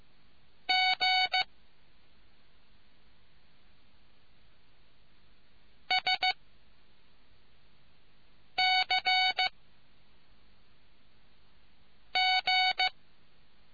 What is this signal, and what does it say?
Morse code, heard as a keyed tone, about 15 wpm, 750 Hz. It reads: G S C G